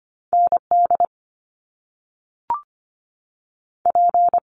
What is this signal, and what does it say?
Morse code, heard as a keyed tone, about 25 wpm, 700 Hz. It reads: ND P
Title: Top Two Letter Combinations in Words 25wpm